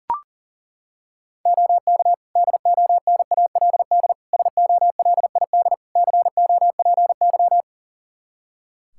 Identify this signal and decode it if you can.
Morse code, heard as a keyed tone, about 40 wpm, 700 Hz. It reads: OK DONALD SOLID COPY
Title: Practice QSO - Exchange 3 - More Info - 40wpm